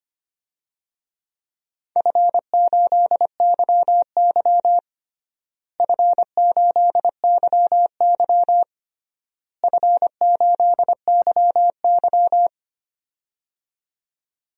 Morse code, heard as a keyed tone, about 25 wpm, 700 Hz. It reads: F8YY F8YY F8YY
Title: DXCC-100 Call Signs - 4 Characters - Repeated 3x - 25wpm